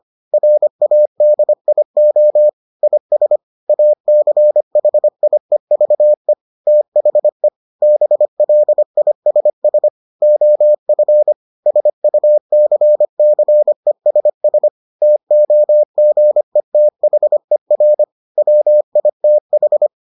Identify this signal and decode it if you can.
Morse code, heard as a keyed tone, about 25 wpm, 600 Hz. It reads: RADIO IS ACHIEVE THE BLISS OF SUCCESS TOGETHER WITH